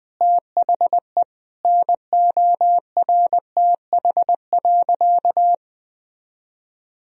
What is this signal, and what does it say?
Morse code, heard as a keyed tone, about 21 wpm, 700 Hz. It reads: THE NORTH.
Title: Sentences from Top 1000 Words - Review Part 1 - 20wpm